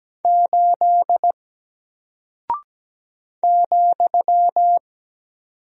Morse code, heard as a keyed tone, about 17 wpm, 700 Hz. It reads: 8 ,
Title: Single Character (Letter:Number:Punctuation:Prosign) Alt 17wpm